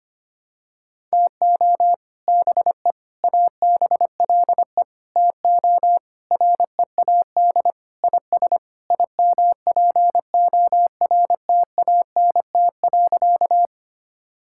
Morse code, heard as a keyed tone, about 25 wpm, 700 Hz. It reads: TO BE ABLE TO READ IS IMPORTANT.
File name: practice / 25wpm / Sentences with 4-Word Phrases - 25wpm